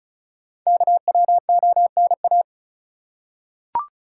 Morse code, heard as a keyed tone, about 36 wpm, 700 Hz. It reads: KWONA
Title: Code Groups - 5 Letters - 35wpm